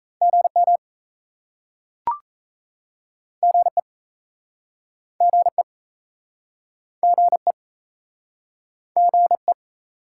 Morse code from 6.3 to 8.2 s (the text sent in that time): GE